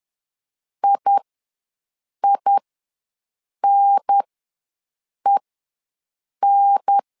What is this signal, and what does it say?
Morse code, heard as a keyed tone, about 10 wpm, spaced as for 6 wpm, 800 Hz. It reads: IINEN
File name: 2 UHDN ws 10wpm